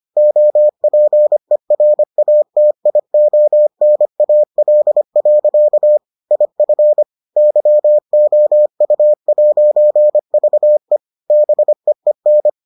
Morse code, heard as a keyed tone, about 25 wpm, 600 Hz. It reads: OPERATIONAL. IF YOU'VE BEEN